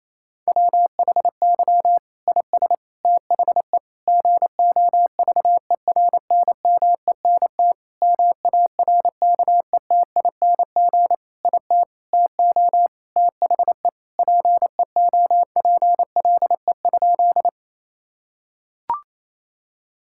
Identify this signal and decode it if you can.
Morse code, heard as a keyed tone, about 28 wpm, 700 Hz. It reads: WHY IS THE GOVERNMENT MARKETING IT TO THE PEOPLE?
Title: Sentences from Top 1000 Words - Review Part 5 - 28wpm